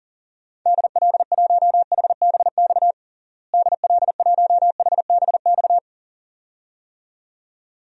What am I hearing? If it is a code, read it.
Morse code, heard as a keyed tone, about 40 wpm, 700 Hz. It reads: DL1HBX DL1HBX
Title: DXCC-100 Call Signs - 6 Characters - Repeated 2x - 40wpm